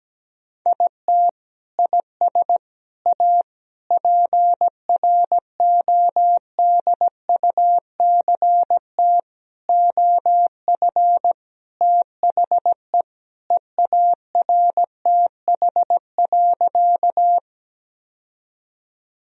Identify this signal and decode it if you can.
Morse code, heard as a keyed tone, about 17 wpm, 700 Hz. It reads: IT IS A PRODUCT OF THE EARTH.
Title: Sentences from Top 1000 Words - Review Part 2 - 17wpm